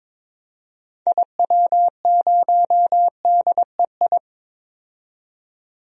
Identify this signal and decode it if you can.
Morse code, heard as a keyed tone, about 21 wpm, 700 Hz. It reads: IW0DEI
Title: DXCC-100 Call Signs - 6 Characters - 22wpm